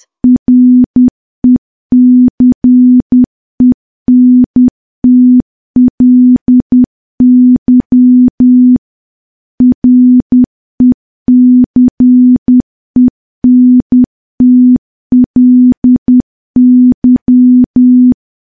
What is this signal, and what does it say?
Morse code, heard as a keyed tone, about 10 wpm, 250 Hz. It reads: RECENTLY RECENTLY
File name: morse / morse-training-10WPM